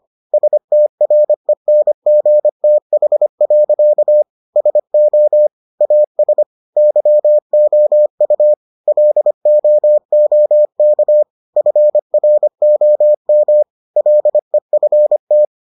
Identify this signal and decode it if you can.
Morse code, heard as a keyed tone, about 25 wpm, 600 Hz. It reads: STRENGTH. SO AS YOU LOOK FROM LEFT